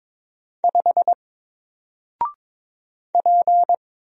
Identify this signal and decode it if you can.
Morse code, heard as a keyed tone, about 22 wpm, 700 Hz. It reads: 5 P
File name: Single-Letter-Number-22wpm